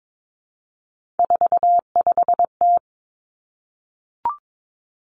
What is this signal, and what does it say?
Morse code, heard as a keyed tone, about 22 wpm, 700 Hz. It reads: 45T
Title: Three Challenging Letter-Number Combinations 22wpm